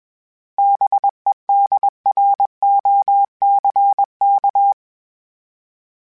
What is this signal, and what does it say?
Morse code, heard as a keyed tone, about 21 wpm, 800 Hz. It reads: BEDROCK
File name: fromSMs / FromSMs mp3s / Cities@20